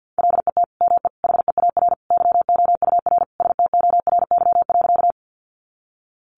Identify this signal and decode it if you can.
Morse code, heard as a keyed tone, about 50 wpm, 700 Hz. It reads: LET ME HEAR YOUR STORY.
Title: Sentences from Top 300 Words 50wpm